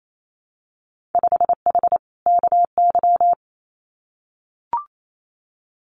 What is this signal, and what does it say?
Morse code, heard as a keyed tone, about 28 wpm, 700 Hz. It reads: 5H KY
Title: Winter Field Day Exchanges Encore 28wpm